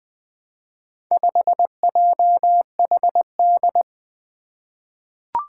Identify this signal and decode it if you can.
Morse code, heard as a keyed tone, about 20 wpm, 700 Hz. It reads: 5JHD
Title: Four Challenging Letter-Number Combinations 20wpm